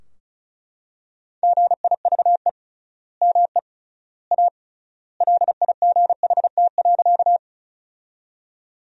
Morse code, heard as a keyed tone, about 34 wpm, 700 Hz. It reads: GIVE ME A LIGHT.
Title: 3x Word-Spacing - Sentences from Top 200 Words - 35wpm